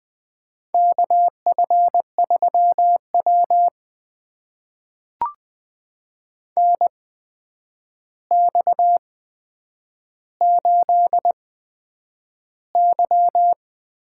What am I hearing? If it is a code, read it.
Morse code, heard as a keyed tone, about 20 wpm, spaced as for 9 wpm, 700 Hz. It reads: KF3W NX8Y
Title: USA Call Signs - 4 Characters - 8x Character Spacing - 20wpm